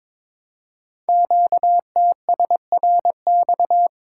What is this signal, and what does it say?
Morse code, heard as a keyed tone, about 23 wpm, 700 Hz. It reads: QTSRX
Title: Code Groups - 5 Letters - 22wpm